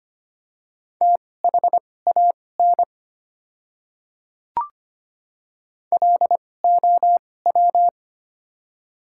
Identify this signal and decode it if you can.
Morse code, heard as a keyed tone, about 25 wpm, spaced as for 18 wpm, 700 Hz. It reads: THAN LOW
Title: 2x Character Spacing - Top 200 Words - 25wpm